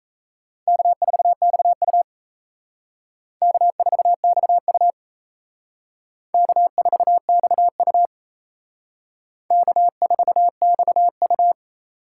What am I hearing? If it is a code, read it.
Morse code, heard as a keyed tone, about 35 wpm, 700 Hz. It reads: K4XU K4XU K4XU K4XU
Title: Speed Racing - USA 1x2 Call Signs 28wpm